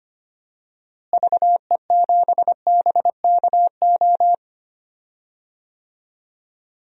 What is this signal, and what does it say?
Morse code, heard as a keyed tone, about 25 wpm, 700 Hz. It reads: VE7BKO
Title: DXCC-100 Call Signs - 6 Characters - Repeated 3x - 25wpm